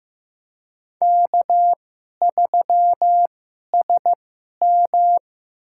Morse code, heard as a keyed tone, about 15 wpm, spaced as for 11 wpm, 700 Hz. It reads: K3SM
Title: USA Call Signs - 4 Characters - 2x Character Spacing - 15wpm